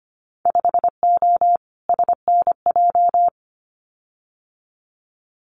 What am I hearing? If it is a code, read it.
Morse code, heard as a keyed tone, about 25 wpm, 690 Hz. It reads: 5O SNJ
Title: Winter Field Day Exchanges Encore 25wpm